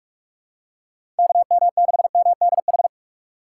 Morse code, heard as a keyed tone, about 47 wpm, 700 Hz. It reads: KM6MDH
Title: USA Call Signs - 6 Characters - 4x Character Spacing - 45wpm